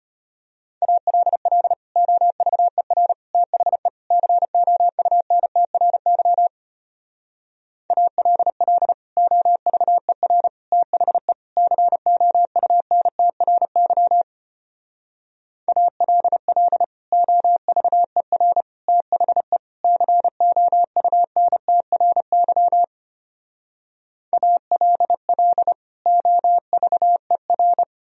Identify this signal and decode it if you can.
Morse code, heard as a keyed tone, about 32 wpm, 700 Hz. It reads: ALL OVER THE COUNTRY ALL OVER THE COUNTRY ALL OVER THE COUNTRY ALL OVER